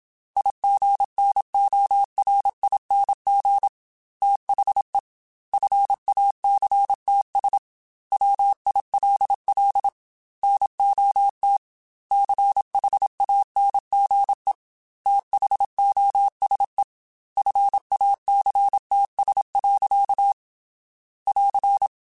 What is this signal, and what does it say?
Morse code, heard as a keyed tone, about 26 wpm, 800 Hz. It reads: IGNORING THE FACTS WILL NOT CHANGE THOSE FACTS. <AR>
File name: Quotes / Quotes-mp3s / IgnoringFacts @25